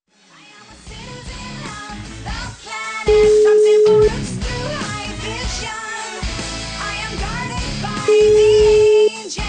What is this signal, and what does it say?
Ringback tone (Europe, 425 Hz).